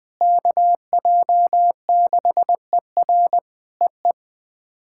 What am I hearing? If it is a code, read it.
Morse code, heard as a keyed tone, about 20 wpm, 700 Hz. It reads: KJ6ER EE
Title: POTA QSO Practice - 20wpm